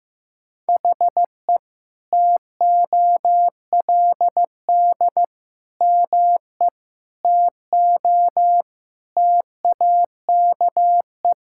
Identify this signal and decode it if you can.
Morse code, heard as a keyed tone, about 16 wpm, 700 Hz. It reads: HE TOLD ME TO TAKE
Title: Sentences from Top 1000 Words - Review Part 6 - 15wpm